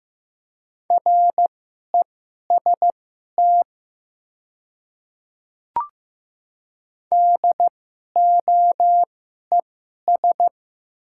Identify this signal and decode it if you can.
Morse code, heard as a keyed tone, about 15 wpm, spaced as for 11 wpm, 690 Hz. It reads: REST DOES
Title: 2x Character Spacing - Top 500 Words - 15wpm